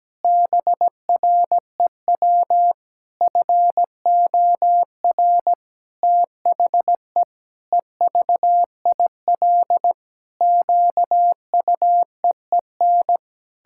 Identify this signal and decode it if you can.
Morse code, heard as a keyed tone, about 17 wpm, 700 Hz. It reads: BREW FOR THE EVIL QUEEN
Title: Pangram Challenge v2 - 17wpm